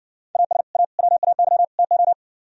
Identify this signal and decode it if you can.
Morse code, heard as a keyed tone, about 57 wpm, 700 Hz. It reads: AS A WAY TO